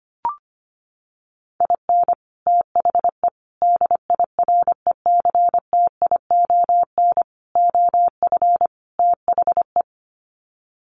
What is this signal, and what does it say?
Morse code, heard as a keyed tone, about 25 wpm, 700 Hz. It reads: IN THE DIRECTION OF THE
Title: Sets of 5 Words 25wpm